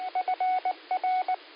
Morse code, heard as a keyed tone, about 19 wpm, 740 Hz. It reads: FR
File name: qsorec SimplexLogic 15-00-51